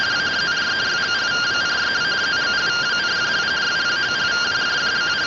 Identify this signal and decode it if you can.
Two-tone teleprinter signal, about 170 Hz shift.